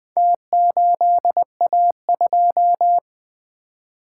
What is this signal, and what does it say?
Morse code, heard as a keyed tone, about 20 wpm, 700 Hz. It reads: T8A2
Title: Four Letter-Number Combinations - 20wpm